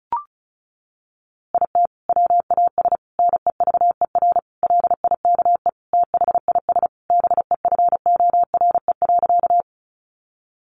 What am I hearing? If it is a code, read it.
Morse code, heard as a keyed tone, about 34 wpm, 700 Hz. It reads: IT WAS NEVER LIKE THIS BEFORE.
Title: A Sound Of Thunder 35wpm